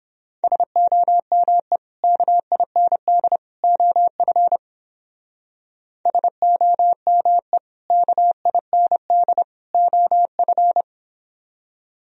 Morse code, heard as a keyed tone, about 31 wpm, 700 Hz. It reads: SOME KIND OF SOME KIND OF